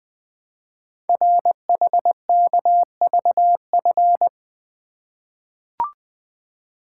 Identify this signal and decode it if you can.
Morse code, heard as a keyed tone, about 20 wpm, 700 Hz. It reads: RHKVF
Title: Code Groups - 5 Letters - 20wpm